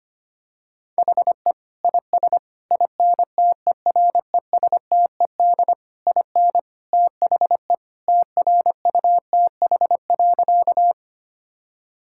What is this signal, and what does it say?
Morse code, heard as a keyed tone, about 25 wpm, 700 Hz. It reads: HE IS INTERESTED IN THE TRUTH.
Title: Sentences from Top 1000 Words - Review Part 6 - 25wpm